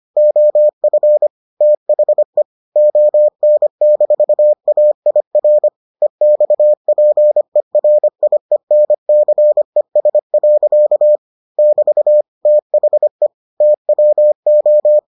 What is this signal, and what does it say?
Morse code, heard as a keyed tone, about 25 wpm, 600 Hz. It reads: OF THE ON-AIR EXPERIENCES. <BT> THE TWO